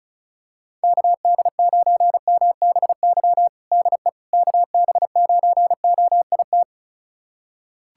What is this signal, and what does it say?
Morse code, heard as a keyed tone, about 36 wpm, 700 Hz. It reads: KD9MBY DE KB9OIT
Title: USA Callsign DE Callsign 35wpm